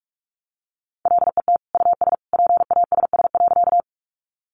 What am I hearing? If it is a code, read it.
Morse code, heard as a keyed tone, about 44 wpm, 700 Hz. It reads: LET US PASS.
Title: Sentences from Top 400 Words 45wpm